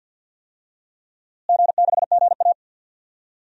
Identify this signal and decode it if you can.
Morse code, heard as a keyed tone, about 50 wpm, 700 Hz. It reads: G6GA
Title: DXCC-100 Call Signs - 4 Characters - Repeated 2x - 50wpm